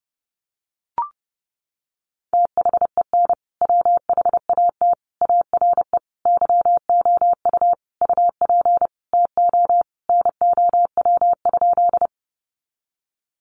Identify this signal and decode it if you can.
Morse code, heard as a keyed tone, about 30 wpm, 700 Hz. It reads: THEN WHAT ARE YOU UP TO NOW?
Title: Sentences from Top 100 Words 30wpm